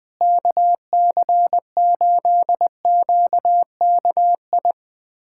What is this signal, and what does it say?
Morse code, heard as a keyed tone, about 21 wpm, 700 Hz. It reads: KC8QKI
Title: USA Callsign DE Callsign 20wpm